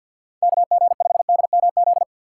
Morse code, heard as a keyed tone, about 53 wpm, 700 Hz. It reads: KG5DMZ
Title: USA Call Signs - 6 Characters - 6x Character Spacing - 50wpm